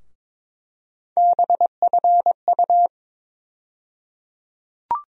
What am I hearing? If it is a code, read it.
Morse code, heard as a keyed tone, about 22 wpm, 700 Hz. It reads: BFU
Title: USA Call Sign Postfixes - 3 Letters 22wpm